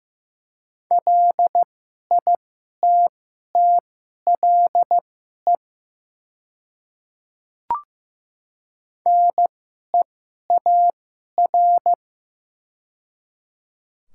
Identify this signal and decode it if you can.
Morse code, heard as a keyed tone, about 15 wpm, spaced as for 11 wpm, 700 Hz. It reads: LITTLE NEAR
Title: 2x Character Spacing - Top 200 Words - 15wpm